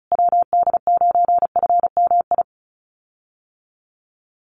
Morse code, heard as a keyed tone, about 34 wpm, 700 Hz. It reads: WD9FMI
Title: USA Call Signs Repeated 3x 35wpm